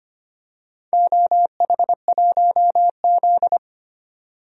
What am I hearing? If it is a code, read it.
Morse code, heard as a keyed tone, about 25 wpm, 700 Hz. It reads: OH1Z
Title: DXCC-100 Call Signs - 4 Characters - 25wpm